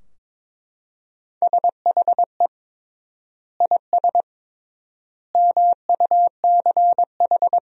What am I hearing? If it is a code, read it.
Morse code, heard as a keyed tone, about 21 wpm, 700 Hz. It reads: SHE IS MUCH
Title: 3x Word-Spacing - Sentences from Top 400 Words - 22wpm